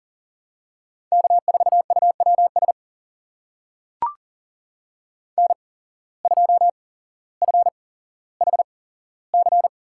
Morse code, heard as a keyed tone, about 40 wpm, spaced as for 17 wpm, 700 Hz. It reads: K4UWS N2FHC
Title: USA Call Signs - 5 Characters - 8x Character Spacing - 40wpm